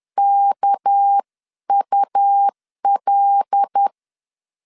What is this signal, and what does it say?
Morse code, heard as a keyed tone, about 10 wpm, 800 Hz. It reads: KUL
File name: LEZIONI / 6 PKX0 / 6 PK 10wpm 2436